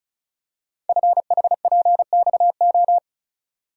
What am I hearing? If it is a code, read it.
Morse code, heard as a keyed tone, about 36 wpm, 690 Hz. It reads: FHPXO
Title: Code Groups - 5 Letters - 35wpm